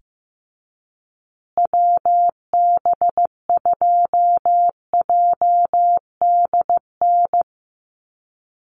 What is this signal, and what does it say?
Morse code, heard as a keyed tone, about 15 wpm, 700 Hz. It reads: WB2JDN
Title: USA Call Signs Repeated 2x 15wpm